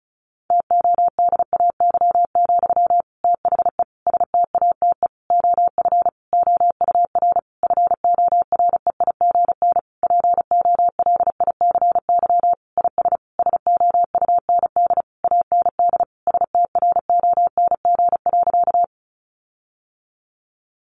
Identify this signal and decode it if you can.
Morse code, heard as a keyed tone, about 34 wpm, 700 Hz. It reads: TODAY, THE STATE OF OUR FOREIGN POLICY IS SOUND AND STRONG.
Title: Sentences from Top 1400 Words 35wpm